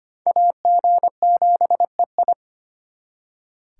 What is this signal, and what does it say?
Morse code, heard as a keyed tone, about 34 wpm, spaced as for 30 wpm, 700 Hz. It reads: AG7EI
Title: Speed Racing - USA 2x2 Call Signs 25wpm